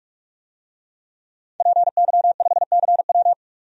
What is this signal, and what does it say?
Morse code, heard as a keyed tone, about 45 wpm, 700 Hz. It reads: PY5CW
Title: DXCC-100 Call Signs - 5 Characters - 45wpm